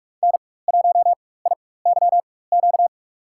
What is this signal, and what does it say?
Morse code, heard as a keyed tone, about 42 wpm, spaced as for 20 wpm, 700 Hz. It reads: N1IYQ
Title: USA Call Signs - 5 Characters - 4x Character Spacing - 45wpm